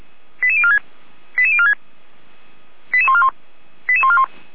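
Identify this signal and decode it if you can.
Five-tone selective call (ZVEI), heard as quick runs of five tones, tones of about 70 ms. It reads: 88046 88046 88141 88141
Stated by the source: ZVEI five-tone selective call received on SDR, from feuerwehr-orlamuende (fire brigade).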